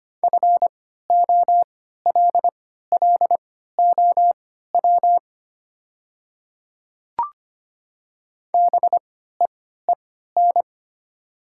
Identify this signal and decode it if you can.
Morse code, heard as a keyed tone, about 25 wpm, spaced as for 14 wpm, 700 Hz. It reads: FOLLOW BEEN
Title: 3x Character Spacing - Top 300 Words - 25wpm